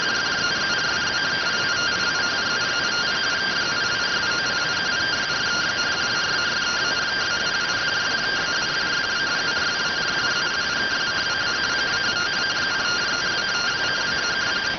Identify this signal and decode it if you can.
Two-tone teleprinter signal, about 170 Hz shift.